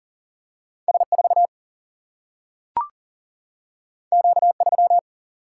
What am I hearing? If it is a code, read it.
Morse code, heard as a keyed tone, about 39 wpm, 700 Hz. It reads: S 4 Q 3
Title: Single-Letter Single-Number 40wpm